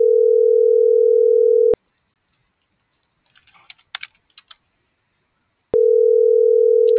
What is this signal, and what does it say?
Ringback tone (North America), 440 + 480 Hz.